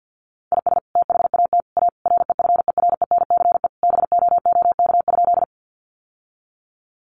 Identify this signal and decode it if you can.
Morse code, heard as a keyed tone, about 48 wpm, 700 Hz. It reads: IS THAT A REFERENCE BOOK?
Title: Sentences from Top 1200 Words 50wpm